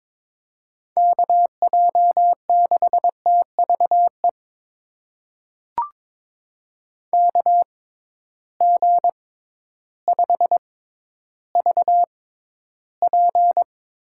Morse code, heard as a keyed tone, about 22 wpm, spaced as for 12 wpm, 700 Hz. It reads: KJ6TVE KG5VP
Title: USA Call Signs - 6 Characters - 6x Character Spacing - 22wpm